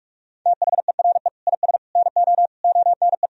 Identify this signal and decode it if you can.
Morse code, heard as a keyed tone, about 42 wpm, 700 Hz. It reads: THERE IS NO ONE
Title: Sentences from Top 200 Words and Abbreviations 45wpm